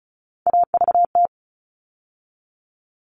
Morse code, heard as a keyed tone, about 34 wpm, 700 Hz. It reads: AVT
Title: All Cut Numbers - 3-Digit 35wpm